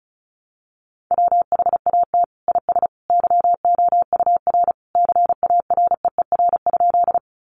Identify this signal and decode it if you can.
Morse code, heard as a keyed tone, about 34 wpm, 700 Hz. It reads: WHAT IS YOUR CAREER?
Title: Sentences from Top 1000 Words 35wpm